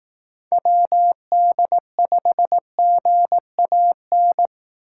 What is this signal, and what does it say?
Morse code, heard as a keyed tone, about 18 wpm, 700 Hz. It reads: WD5GAN
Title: Speed Racing - USA 2x3 Call Signs 15wpm